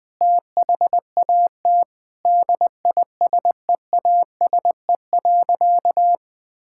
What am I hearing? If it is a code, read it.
Morse code, heard as a keyed tone, about 21 wpm, 700 Hz. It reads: THAT DISEASE.
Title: Sentences from Top 1000 Words - Review Part 6 - 20wpm